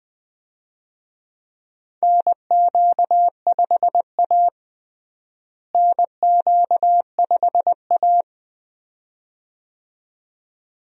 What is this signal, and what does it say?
Morse code, heard as a keyed tone, about 20 wpm, 700 Hz. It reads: NQ5A NQ5A
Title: USA Call Signs - 4 Characters - Repeated 2x - 20wpm